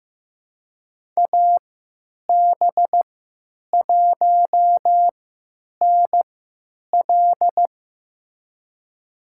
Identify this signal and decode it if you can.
Morse code, heard as a keyed tone, about 15 wpm, spaced as for 9 wpm, 700 Hz. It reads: AB1NL
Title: USA Call Signs - 5 Characters - 3x Character Spacing - 15wpm